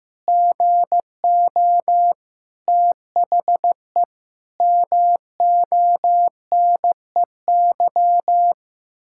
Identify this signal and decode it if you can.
Morse code, heard as a keyed tone, about 15 wpm, 700 Hz. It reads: GO THE MONEY
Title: Pangram Challenge v2 - 15wpm